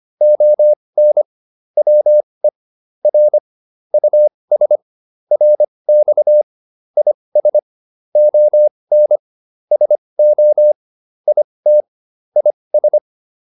Morse code, heard as a keyed tone, about 25 wpm, spaced as for 20 wpm, 600 Hz. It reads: ON WE R US RX IS ON SO IT IS